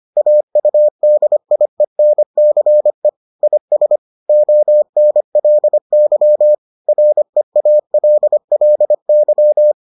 Morse code, heard as a keyed tone, about 25 wpm, 600 Hz. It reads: AUDIENCE IS ONLY REALLY